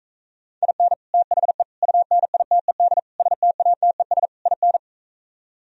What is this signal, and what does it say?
Morse code, heard as a keyed tone, about 44 wpm, 700 Hz. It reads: IN THE UNITED STATES IN